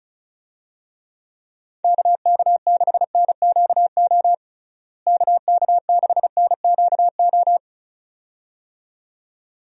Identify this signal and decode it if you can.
Morse code, heard as a keyed tone, about 34 wpm, 700 Hz. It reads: KK6NQO KK6NQO
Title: USA Call Signs - 6 Characters - Repeated 2x - 35wpm